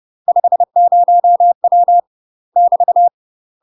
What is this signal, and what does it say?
Morse code, heard as a keyed tone, about 30 wpm, 700 Hz. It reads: 50W <BT>
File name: UK MASTER QSOs TRAINING / 30WPM / UK MASTER QSOs 0014